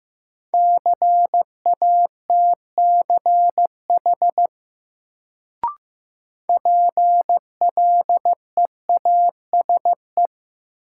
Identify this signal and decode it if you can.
Morse code, heard as a keyed tone, about 15 wpm, 700 Hz. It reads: CATCH PLEASE
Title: Sentences from Top 1500 Words - New Word Drill Alt - 15wpm